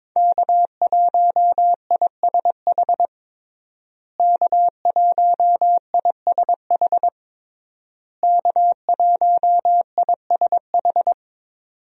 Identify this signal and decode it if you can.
Morse code, heard as a keyed tone, about 22 wpm, 700 Hz. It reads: K1ISH K1ISH K1ISH
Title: USA Call Signs - 5 Characters - Repeated 3x - 22wpm